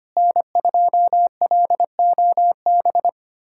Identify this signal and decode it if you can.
Morse code, heard as a keyed tone, about 25 wpm, 700 Hz. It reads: N2LOB
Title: USA Callsign DE Callsign - 5 Characters - 25wpm